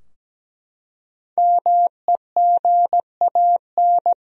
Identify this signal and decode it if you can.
Morse code, heard as a keyed tone, about 17 wpm, 700 Hz. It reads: MEGAN
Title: USA Common Names - 17wpm